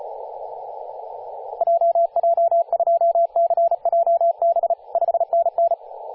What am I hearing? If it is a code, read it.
Morse code, heard as a keyed tone, about 35 wpm, 680 Hz, with radio noise. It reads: JJ2CJB 5NN
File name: posts / 2016-sep-22-0537z-kh0-kw2x-18-077mhz-cw